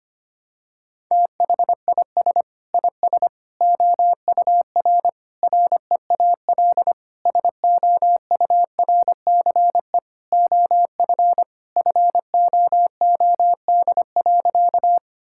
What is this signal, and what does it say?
Morse code, heard as a keyed tone, about 25 wpm, 700 Hz. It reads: THIS IS OUR REAL SOURCE OF FOOD.
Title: Sentences from Top 1000 Words - Review Part 5 - 25wpm